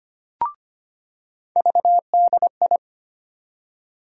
Morse code, heard as a keyed tone, about 25 wpm, 700 Hz. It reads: VDI
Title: USA Call Sign Postfixes - 3 Letters 25wpm